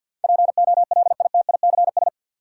Morse code, heard as a keyed tone, about 52 wpm, 700 Hz. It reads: POLITICS
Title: Sentences from Top 1000 Words - New Word Drill - 50wpm